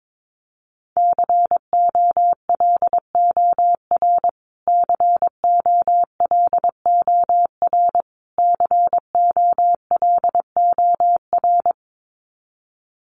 Morse code, heard as a keyed tone, about 22 wpm, 700 Hz. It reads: COLOR COLOR COLOR
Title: Top 500 Words Repeated 22wpm